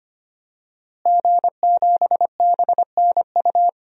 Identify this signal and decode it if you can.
Morse code, heard as a keyed tone, about 25 wpm, 700 Hz. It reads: G7BNU
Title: DXCC-100 Call Signs - 5 Characters - Repeated 3x - 25wpm